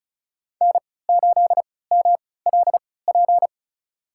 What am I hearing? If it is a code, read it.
Morse code, heard as a keyed tone, about 34 wpm, spaced as for 20 wpm, 690 Hz. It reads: N8MLP
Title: USA Call Signs - 3x Character Spacing - 35wpm